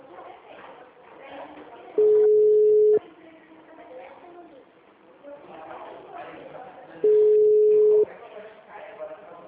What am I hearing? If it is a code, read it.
Ringback tone (Europe, 425 Hz), with speech.